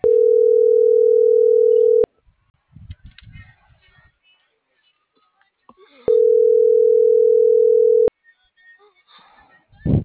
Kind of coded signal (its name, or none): telephone call progress tone